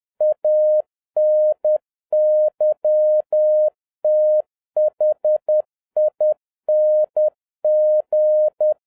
Morse code, heard as a keyed tone, about 10 wpm, 610 Hz. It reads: ANYTHING